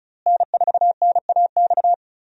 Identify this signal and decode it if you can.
Morse code, heard as a keyed tone, about 36 wpm, 700 Hz. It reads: N4NAX
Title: USA Call Signs - 5 Characters - Repeated 3x - 35wpm